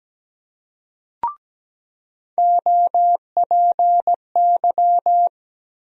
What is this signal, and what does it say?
Morse code, heard as a keyed tone, about 17 wpm, 700 Hz. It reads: OPY
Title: USA Call Sign Postfixes - 3 Letters 17wpm